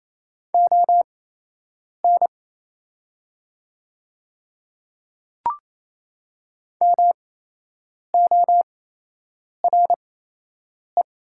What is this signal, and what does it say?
Morse code, heard as a keyed tone, about 28 wpm, spaced as for 8 wpm, 700 Hz. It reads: ON MORE
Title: 8x Character Spacing - Top 100 Words - 28wpm